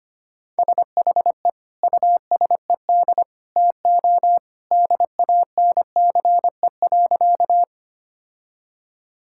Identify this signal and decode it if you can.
Morse code, heard as a keyed tone, about 25 wpm, 700 Hz. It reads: SHE USED TO DANCE.
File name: practice / 25wpm / Sentences with 2-Word Phrases - 25wpm